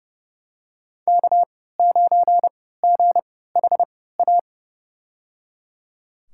Morse code, heard as a keyed tone, about 31 wpm, spaced as for 17 wpm, 700 Hz. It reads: K9GHA
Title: USA Call Signs - 3x Character Spacing - 30wpm